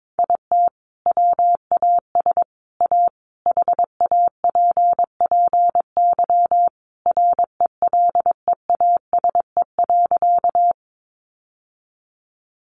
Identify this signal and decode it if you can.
Morse code, heard as a keyed tone, about 22 wpm, 700 Hz. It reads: IT WAS A HAPPY RELEASE.
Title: Sentences from Top 1100 Words 22wpm